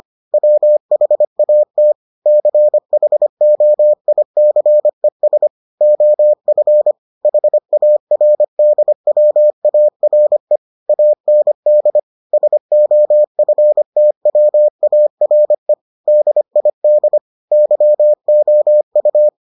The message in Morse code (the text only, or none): WHAT CHOICES OF HARDWARE AND SOFTWARE DID YOU